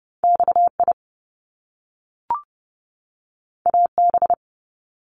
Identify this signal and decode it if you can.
Morse code, heard as a keyed tone, about 30 wpm, 700 Hz. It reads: XI AB
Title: Top Two Letter Combinations in Words 30wpm